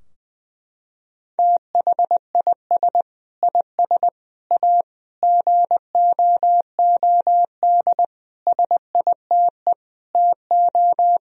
Morse code, heard as a keyed tone, about 20 wpm, 700 Hz. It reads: THIS IS A GOOD SITE TO
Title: Sentences from Top 1000 Words - Review Part 4 - 20wpm